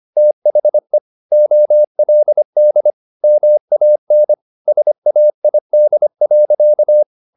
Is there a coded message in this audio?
Yes: Morse code from 0.2 to 7.0 s, heard as a keyed tone, about 25 wpm, 600 Hz. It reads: THE OLD MAN SAID.